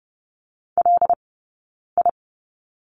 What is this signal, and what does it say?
Morse code, heard as a keyed tone, about 30 wpm, 700 Hz. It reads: L I
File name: course / 30:10wpm / 086 Lesson 15d - Number 5 - Two Characters 30wpm